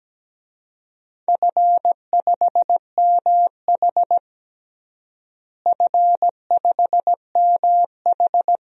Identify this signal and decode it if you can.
Morse code, heard as a keyed tone, about 17 wpm, 700 Hz. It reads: F5MH F5MH
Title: DXCC-100 Call Signs - 4 Characters - Repeated 3x - 17wpm